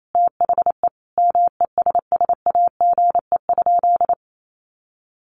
Morse code, heard as a keyed tone, about 28 wpm, 700 Hz. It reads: THE MESSAGE?
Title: Sentences from Top 1100 Words 28wpm